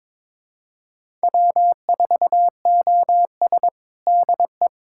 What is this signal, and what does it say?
Morse code, heard as a keyed tone, about 22 wpm, 700 Hz. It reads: W4OS DE
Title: USA Callsign DE Callsign 22wpm